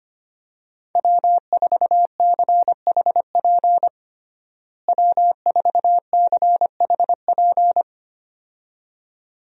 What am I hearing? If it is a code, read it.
Morse code, heard as a keyed tone, about 25 wpm, 700 Hz. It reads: W4CHP W4CHP
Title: USA Call Signs - 5 Characters - Repeated 2x - 25wpm